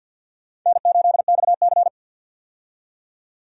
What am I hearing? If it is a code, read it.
Morse code, heard as a keyed tone, about 48 wpm, 700 Hz. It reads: N8XC
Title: USA Call Signs - 4 Characters - 50wpm